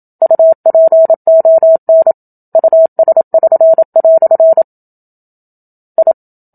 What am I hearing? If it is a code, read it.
Morse code, heard as a keyed tone, about 27 wpm, 650 Hz. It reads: UPON US<SN>" I
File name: Book Two / 01-Chapter 1 - 27 WPM